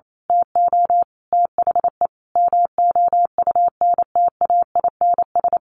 Morse code, heard as a keyed tone, about 28 wpm, 700 Hz. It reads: TO THE MOUNTAINS
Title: A Sound Of Thunder 28wpm